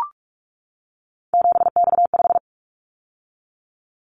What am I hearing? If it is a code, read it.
Morse code, heard as a keyed tone, about 44 wpm, 700 Hz. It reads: 7X5
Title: Callsign Prefixes for Top 100 DXCC - 45wpm